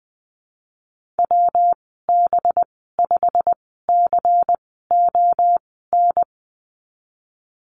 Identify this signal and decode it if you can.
Morse code, heard as a keyed tone, about 20 wpm, spaced as for 15 wpm, 700 Hz. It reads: WB5CON
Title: 193 Lesson 31g - Letter v - Call Signs 20wpm